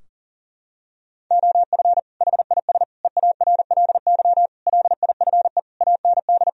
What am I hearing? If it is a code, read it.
Morse code, heard as a keyed tone, about 40 wpm, 700 Hz. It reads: OF HIS EARLY LIFE AND
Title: Sentences from Top 1000 Words - Review Part 4 - 40wpm